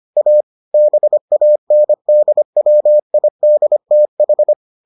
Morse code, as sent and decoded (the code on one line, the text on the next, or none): .- / -... .- -. -.. .-- .. -.. - ....
A BANDWIDTH